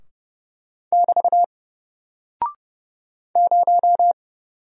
Morse code, heard as a keyed tone, about 29 wpm, 700 Hz. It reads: <BT> 0
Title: Single Character (Letter:Number:Punctuation:Prosign) 30wpm